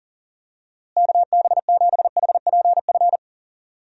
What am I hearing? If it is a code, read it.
Morse code, heard as a keyed tone, about 40 wpm, 700 Hz. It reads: KB7HPF
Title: USA Call Signs - 6 Characters - Repeated 2x - 40wpm